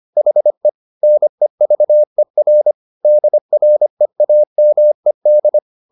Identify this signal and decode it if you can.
Morse code, heard as a keyed tone, about 25 wpm, 600 Hz. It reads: HE NEVER DREAMED
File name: the old man and the sea part 1 25wpm